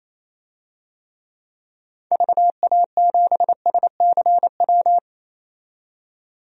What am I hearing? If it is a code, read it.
Morse code, heard as a keyed tone, about 28 wpm, 700 Hz. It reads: VA7SCW
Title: DXCC-100 Call Signs - 6 Characters - 28wpm